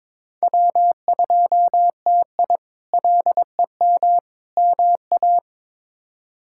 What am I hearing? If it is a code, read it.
Morse code, heard as a keyed tone, about 22 wpm, 700 Hz. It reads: W2TI LEM MA
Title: K1USN SST QSO Practice - 22wpm